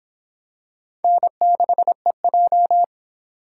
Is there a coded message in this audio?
Yes: Morse code from 1.0 to 2.8 s, heard as a keyed tone, about 25 wpm, 700 Hz. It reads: N6EJ